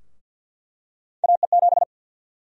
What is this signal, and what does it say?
Morse code, heard as a keyed tone, about 46 wpm, 700 Hz. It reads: AE7
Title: USA Call Sign Prefixes - 50wpm